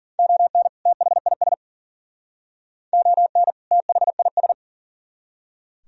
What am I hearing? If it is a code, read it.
Morse code, heard as a keyed tone, about 51 wpm, 700 Hz. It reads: ON THIS ON THIS